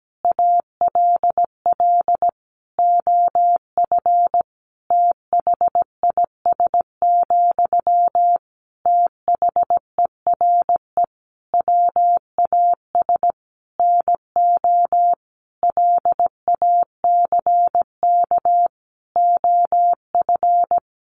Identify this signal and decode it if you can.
Morse code, heard as a keyed tone, about 17 wpm, 700 Hz. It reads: ALL OF THIS, THERE WAS NO LACK OF
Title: Sentences from Top 1300 Words 17wpm